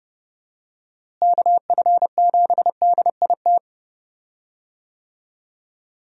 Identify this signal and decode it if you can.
Morse code, heard as a keyed tone, about 30 wpm, 700 Hz. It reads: KF7DIT
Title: USA Call Signs - 6 Characters - Repeated 3x - 30wpm